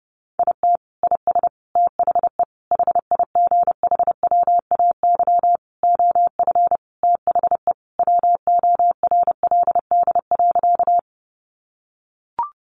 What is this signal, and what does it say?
Morse code, heard as a keyed tone, about 30 wpm, 700 Hz. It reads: IT IS THE HIGHWAY OF THE WORLD.
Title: Sentences from Top 1400 Words 30wpm